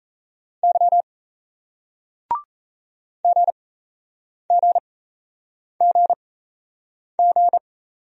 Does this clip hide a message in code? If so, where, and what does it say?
Yes: Morse code from 0.6 to 7.6 s, heard as a keyed tone, about 37 wpm, spaced as for 8 wpm, 700 Hz. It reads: Y GGGG